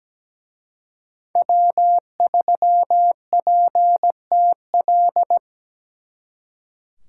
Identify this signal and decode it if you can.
Morse code, heard as a keyed tone, about 20 wpm, 700 Hz. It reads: W3PTL
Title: Speed Racing - USA 1x3 Call Signs 17wpm